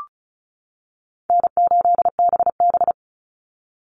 Morse code, heard as a keyed tone, about 34 wpm, 700 Hz. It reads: N8BB
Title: NA SOTA Chaser Callsigns 35wpm